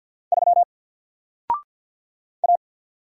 Morse code, heard as a keyed tone, about 48 wpm, 700 Hz. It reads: 3 A
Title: Single Character (Letter:Number:Punctuation:Prosign) 50wpm